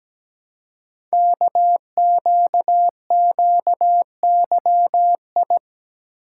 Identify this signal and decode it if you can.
Morse code, heard as a keyed tone, about 17 wpm, 700 Hz. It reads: KQQYI
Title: Code Groups - 5 Letter:Numbers - 17wpm